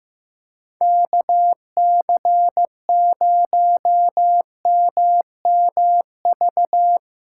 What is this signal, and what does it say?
Morse code, heard as a keyed tone, about 16 wpm, 700 Hz. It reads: KC0MMV
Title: Speed Racing - USA 2x3 Call Signs 15wpm